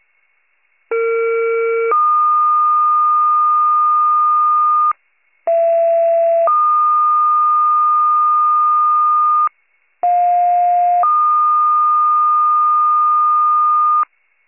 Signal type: two-tone page

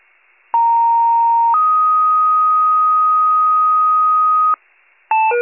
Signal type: two-tone page

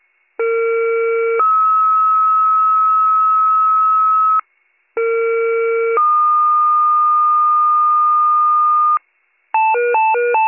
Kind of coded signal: two-tone page

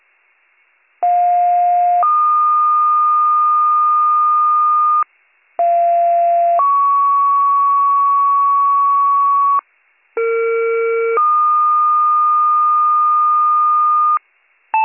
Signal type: two-tone page

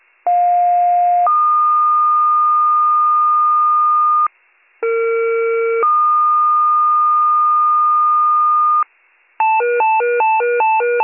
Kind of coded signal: two-tone page

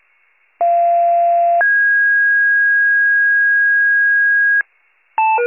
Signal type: two-tone page

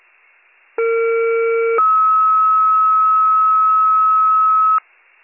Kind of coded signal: two-tone page

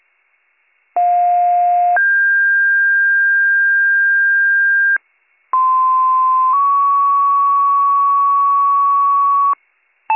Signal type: two-tone page

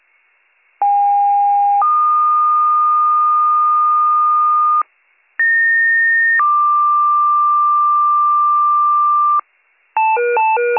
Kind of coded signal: two-tone page

